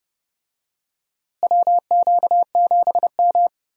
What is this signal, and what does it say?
Morse code, heard as a keyed tone, about 30 wpm, 700 Hz. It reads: WQ7M
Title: USA Call Signs - 4 Characters - Repeated 3x - 30wpm